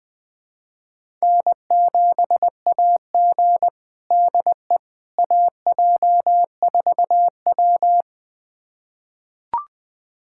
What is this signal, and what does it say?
Morse code, heard as a keyed tone, about 21 wpm, 700 Hz. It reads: N7AG DE AJ4W
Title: USA Callsign DE Callsign - 4 Characters - 20wpm